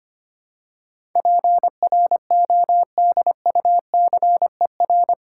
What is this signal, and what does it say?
Morse code, heard as a keyed tone, about 25 wpm, 700 Hz. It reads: PRODUCER
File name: Eight Letter Words - 25wpm